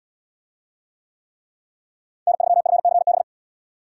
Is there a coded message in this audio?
Yes: Morse code from 2.3 to 3.2 s, heard as a keyed tone, about 77 wpm, 700 Hz. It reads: N3FCB